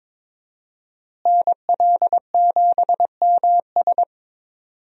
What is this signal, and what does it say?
Morse code, heard as a keyed tone, about 22 wpm, 700 Hz. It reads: NL7MS
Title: USA Call Signs - 5 Characters - Repeated 3x - 22wpm